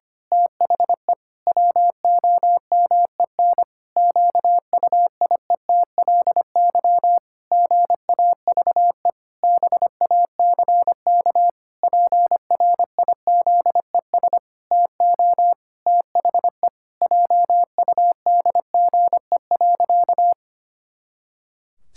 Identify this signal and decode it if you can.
Morse code, heard as a keyed tone, about 25 wpm, 700 Hz. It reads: THE WOMEN QUIETLY GAVE BACK PRIZES TO THE JUDGE.
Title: Pangram Challenge v2 - 25wpm